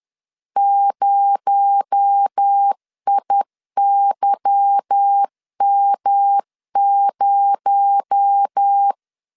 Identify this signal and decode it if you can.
Morse code, heard as a keyed tone, about 10 wpm, 800 Hz. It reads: 0IYM0
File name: LEZIONI / 8 YJ45 / 8 Y 10wpm 2615